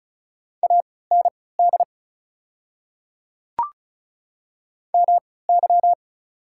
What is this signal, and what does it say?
Morse code, heard as a keyed tone, about 34 wpm, spaced as for 20 wpm, 690 Hz. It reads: AND MY
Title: 3x Character Spacing - Top 100 Words - 35wpm